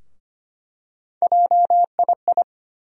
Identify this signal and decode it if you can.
Morse code, heard as a keyed tone, about 25 wpm, 700 Hz. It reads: JII
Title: USA Call Sign Postfixes - 3 Letters 25wpm